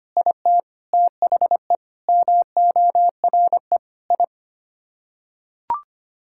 Morse code, heard as a keyed tone, about 25 wpm, 700 Hz. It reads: IT THE MORE I
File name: Sets of 9 Words - 25wpm